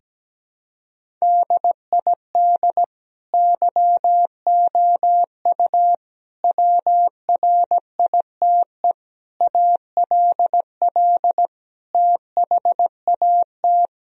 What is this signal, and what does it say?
Morse code, heard as a keyed tone, about 17 wpm, 700 Hz. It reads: DID YOU WRITE ALL THAT
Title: Sentences from Top 1000 Words - Review Part 1 - 17wpm